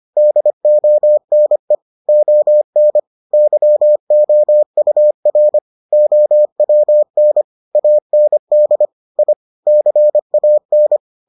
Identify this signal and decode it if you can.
Morse code, heard as a keyed tone, about 25 wpm, 600 Hz. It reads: DONE ON YOUR OWN AND I CAN